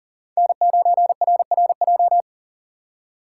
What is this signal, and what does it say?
Morse code, heard as a keyed tone, about 40 wpm, 690 Hz. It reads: N9RRJ
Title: USA Call Signs - 5 Characters - 3x Character Spacing - 40wpm